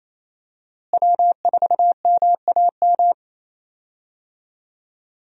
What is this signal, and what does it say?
Morse code, heard as a keyed tone, about 27 wpm, 700 Hz. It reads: W4MAM
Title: USA Call Signs - 5 Characters - Repeated 2x - 28wpm